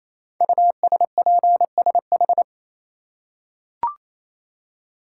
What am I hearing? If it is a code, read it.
Morse code, heard as a keyed tone, about 27 wpm, 700 Hz. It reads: USPSH
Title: Code Groups - 5 Letters - 28wpm